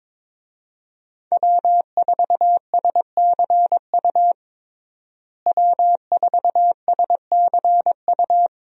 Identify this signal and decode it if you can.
Morse code, heard as a keyed tone, about 21 wpm, 700 Hz. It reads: W4SCU W4SCU
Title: USA Call Signs - 5 Characters - Repeated 2x - 22wpm